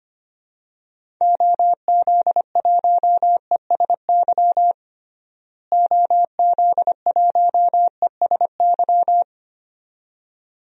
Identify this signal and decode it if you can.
Morse code, heard as a keyed tone, about 25 wpm, 700 Hz. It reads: OZ1ESY OZ1ESY
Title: DXCC-100 Call Signs - 6 Characters - Repeated 2x - 25wpm